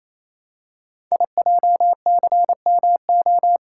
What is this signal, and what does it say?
Morse code, heard as a keyed tone, about 27 wpm, 700 Hz. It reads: IJCMO
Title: Code Groups - 5 Letters - 28wpm